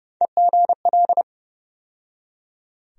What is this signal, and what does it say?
Morse code, heard as a keyed tone, about 31 wpm, 700 Hz. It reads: EGL
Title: USA Call Sign Postfixes - 3 Letters 30wpm